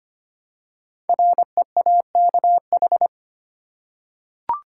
Morse code, heard as a keyed tone, about 25 wpm, 700 Hz. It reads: REAKH
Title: Code Groups - 5 Letter:Numbers - 25wpm